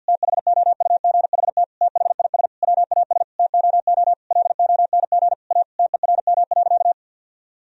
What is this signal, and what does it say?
Morse code, heard as a keyed tone, about 50 wpm, 700 Hz. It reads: THOUGHT THIS WAS TOO LONG A TERM.
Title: Sentences from Top 1000 Words - Review Part 4 - 50wpm